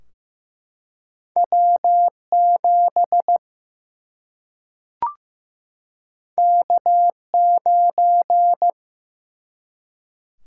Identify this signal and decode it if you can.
Morse code, heard as a keyed tone, about 15 wpm, 700 Hz. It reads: W 7 K 9
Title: Single-Letter Single-Number 15wpm